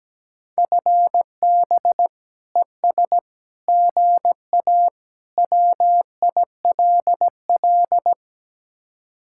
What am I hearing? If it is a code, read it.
Morse code, heard as a keyed tone, about 17 wpm, 700 Hz. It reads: FB ES GA WILL
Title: Speed-Racing Common QSO Phrases Encore - 17wpm